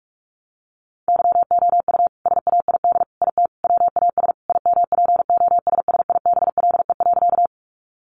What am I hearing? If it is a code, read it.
Morse code, heard as a keyed tone, about 44 wpm, 700 Hz. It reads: YOU SAID IT WAS IMPOSSIBLE.
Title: Sentences from Top 1100 Words 45wpm